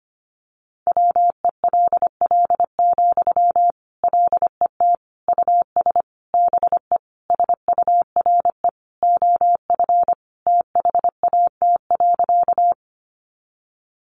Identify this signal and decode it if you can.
Morse code, heard as a keyed tone, about 25 wpm, 700 Hz. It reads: WELL, LET US BE SURE OF THAT.
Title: Sentences from Top 300 Words 25wpm